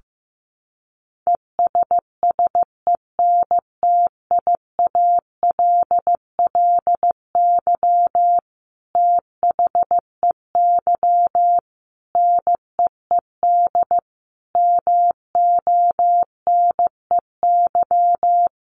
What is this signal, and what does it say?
Morse code, heard as a keyed tone, about 14 wpm, 700 Hz. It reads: ESSENTIALLY THEY NEED MONEY
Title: Sentences from Top 1300 Words 15wpm